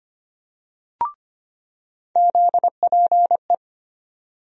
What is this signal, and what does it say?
Morse code, heard as a keyed tone, about 25 wpm, 700 Hz. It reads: ZPE
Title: USA Call Sign Postfixes - 3 Letters 25wpm